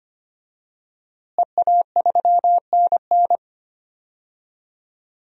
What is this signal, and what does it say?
Morse code, heard as a keyed tone, about 25 wpm, 700 Hz. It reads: EA3NN
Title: DXCC-100 Call Signs - 5 Characters - Repeated 2x - 25wpm